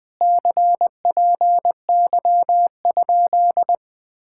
Morse code, heard as a keyed tone, about 20 wpm, 700 Hz. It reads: CPY?
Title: Speed-Racing Common QSO Phrases - 17wpm